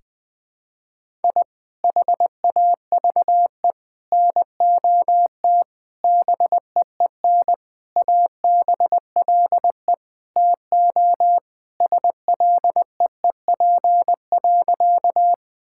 Morse code, heard as a keyed tone, about 21 wpm, 700 Hz. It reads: I HAVE NOT BEEN ABLE TO SLEEP.
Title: Sentences with 5-Word Phrases - Spoken Beginning - 20wpm